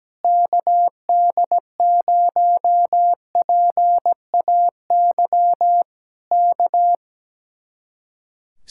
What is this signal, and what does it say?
Morse code, heard as a keyed tone, about 17 wpm, 700 Hz. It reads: KD0PAY K
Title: Speed-Racing Common QSO Phrases - 17wpm